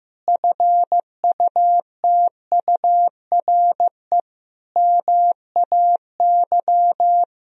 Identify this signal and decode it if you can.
Morse code, heard as a keyed tone, about 15 wpm, 700 Hz. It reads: FUTURE MAY
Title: Sentences from Top 1000 Words - Review Part 2 - 15wpm